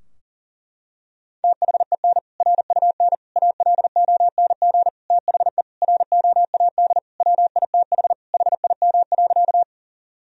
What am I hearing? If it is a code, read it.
Morse code, heard as a keyed tone, about 40 wpm, 700 Hz. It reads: THEN RUN ALONG THE ROAD WITH HIM.
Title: Sentences from Top 1000 Words - Review Part 2 - 40wpm